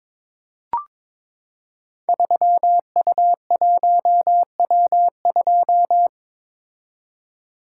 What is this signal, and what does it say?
Morse code, heard as a keyed tone, about 22 wpm, 700 Hz. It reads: 3U1W2
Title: Code Groups - 5 Letter:Numbers - 22wpm